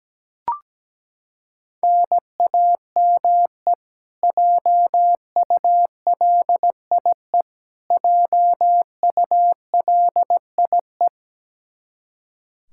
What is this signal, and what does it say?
Morse code, heard as a keyed tone, about 17 wpm, 700 Hz. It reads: NAME JULIE JULIE
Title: Practice QSO - Exchange 1 - Beginning - 17wpm